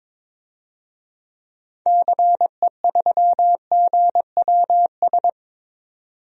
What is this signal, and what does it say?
Morse code, heard as a keyed tone, about 21 wpm, 700 Hz. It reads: CE3GWS
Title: DXCC-100 Call Signs - 6 Characters - 22wpm